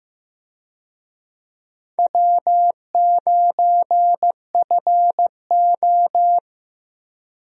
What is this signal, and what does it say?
Morse code, heard as a keyed tone, about 15 wpm, 700 Hz. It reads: W9FO
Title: USA Call Signs - 4 Characters - Repeated 3x - 15wpm